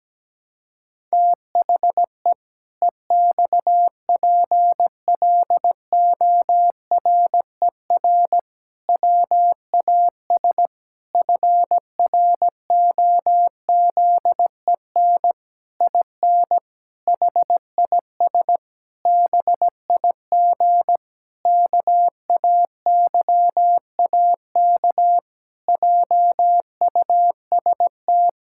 Morse code, heard as a keyed tone, about 17 wpm, 700 Hz. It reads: THE EXPLORER WAS FROZEN IN HIS BIG KAYAK JUST
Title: Pangram Challenge v2 - 17wpm